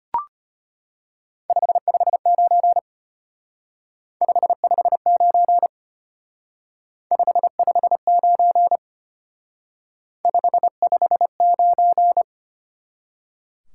Morse code, heard as a keyed tone, about 32 wpm, 700 Hz. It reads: 559 559 559 559